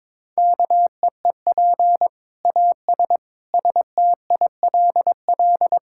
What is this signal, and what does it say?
Morse code, heard as a keyed tone, about 22 wpm, 700 Hz. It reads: KEEP AS STILL
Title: Sentences from Top 1000 Words - Review Part 5 - 22wpm